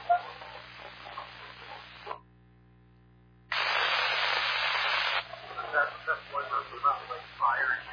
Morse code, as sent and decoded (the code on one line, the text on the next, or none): none